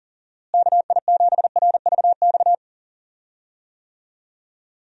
Morse code, heard as a keyed tone, about 40 wpm, 700 Hz. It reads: KI7RVX
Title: USA Call Signs - 6 Characters - Repeated 2x - 40wpm